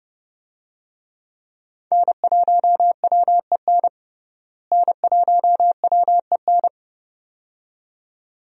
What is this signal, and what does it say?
Morse code, heard as a keyed tone, about 32 wpm, 700 Hz. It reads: N1WEN N1WEN
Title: USA Call Signs - 5 Characters - Repeated 2x - 30wpm